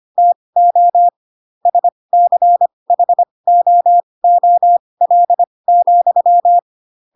Morse code, heard as a keyed tone, about 25 wpm, spaced as for 20 wpm, 700 Hz. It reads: TO SCHOOL,
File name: Pinocchio / Pinocchio ch4 15 25 ik1hge